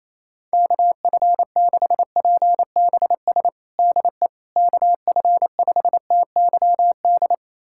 Morse code, heard as a keyed tone, about 28 wpm, 700 Hz. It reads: KF6PBS DE KF5TYD
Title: USA Callsign DE Callsign - 6 Characters - 28wpm